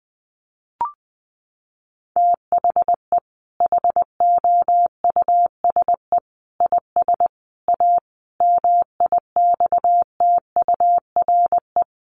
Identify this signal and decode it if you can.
Morse code, heard as a keyed tone, about 20 wpm, 700 Hz. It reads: THE HOUSE IS A MIXTURE
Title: Sentences from Top 1500 Words 20wpm